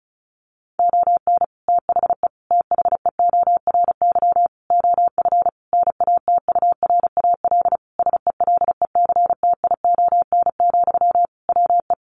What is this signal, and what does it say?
Morse code, heard as a keyed tone, about 34 wpm, 700 Hz. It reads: ON THE THEORY OF NATURAL SELECTION, WE
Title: Sentences from Top 1400 Words 35wpm